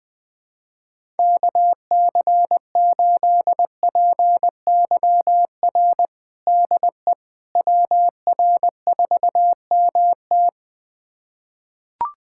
Morse code, heard as a keyed tone, about 20 wpm, 700 Hz. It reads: KC8PYR DE WR4MT
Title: USA Callsign DE Callsign 20wpm